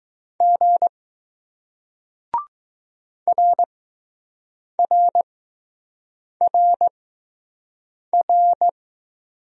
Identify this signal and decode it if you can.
Morse code, heard as a keyed tone, about 20 wpm, spaced as for 6 wpm, 700 Hz. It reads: G RRRR